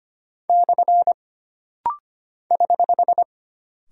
Morse code, heard as a keyed tone, about 25 wpm, 700 Hz. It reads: / <HH>
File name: Single Punctuation-Prosign - Rapid-Fire - 25wpm